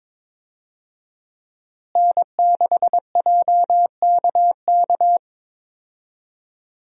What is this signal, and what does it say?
Morse code, heard as a keyed tone, about 22 wpm, 700 Hz. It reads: N6JKK
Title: USA Call Signs - 5 Characters - Repeated 3x - 22wpm